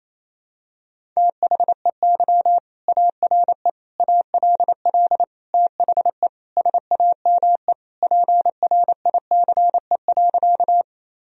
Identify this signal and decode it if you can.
Morse code, heard as a keyed tone, about 28 wpm, 700 Hz. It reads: THEY ARE ALL THE SAME PRICE.
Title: Sentences from Top 1000 Words - Review Part 4 - 28wpm